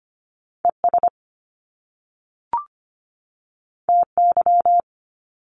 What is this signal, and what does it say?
Morse code, heard as a keyed tone, about 25 wpm, 700 Hz. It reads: ES TY
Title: Top Two Letter Combinations in Words 25wpm